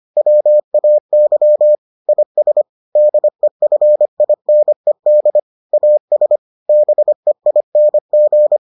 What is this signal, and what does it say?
Morse code, heard as a keyed tone, about 25 wpm, 600 Hz. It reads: WAY IS DEFINED AS BEING